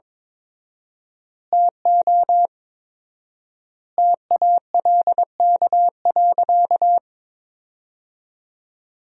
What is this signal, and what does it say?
Morse code, heard as a keyed tone, about 22 wpm, 700 Hz. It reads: TO TALK.
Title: 4x Word-Spacing - Sentences from Top 400 Words - 22wpm